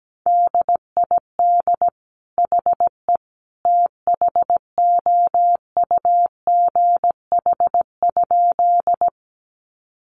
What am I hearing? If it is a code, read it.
Morse code, heard as a keyed tone, about 17 wpm, 700 Hz. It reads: DID HE THOUGH?
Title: Sentences from Top 400 Words 17wpm